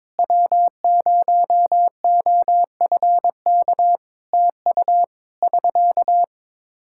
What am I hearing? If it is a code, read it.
Morse code, heard as a keyed tone, about 22 wpm, 700 Hz. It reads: W0OFK TU <SK>
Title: K1USN SST QSO Practice 22wpm